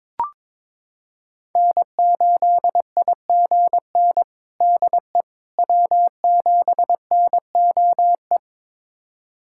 Morse code, heard as a keyed tone, about 21 wpm, 700 Hz. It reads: N8IGN DE W7NOE
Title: USA Callsign DE Callsign - 5 Characters - 22wpm